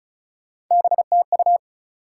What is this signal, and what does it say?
Morse code, heard as a keyed tone, about 36 wpm, 700 Hz. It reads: BTU
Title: Top 180 QSO Elements 35wpm